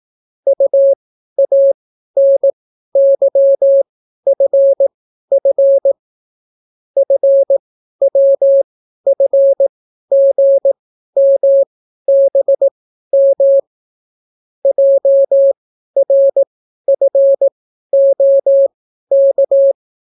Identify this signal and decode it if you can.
Morse code, heard as a keyed tone, about 18 wpm, spaced as for 12 wpm, 550 Hz. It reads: UANYFF FWFGMBM JRFOK